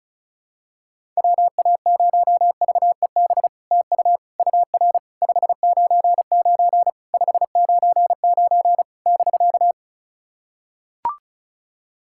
Morse code, heard as a keyed tone, about 36 wpm, 690 Hz. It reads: WA0VEB TU UR 599 599 <BK>
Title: SOTA QSO Practice 35wpm